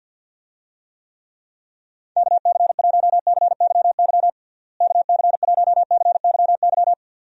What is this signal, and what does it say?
Morse code, heard as a keyed tone, about 45 wpm, 700 Hz. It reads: KC1CYY KC1CYY
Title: USA Call Signs - 6 Characters - Repeated 2x - 50wpm